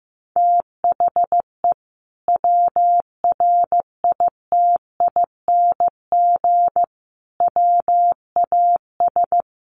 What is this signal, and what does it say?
Morse code, heard as a keyed tone, about 14 wpm, 700 Hz. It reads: THE WRITING WAS
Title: Sentences from Top 1100 Words 15wpm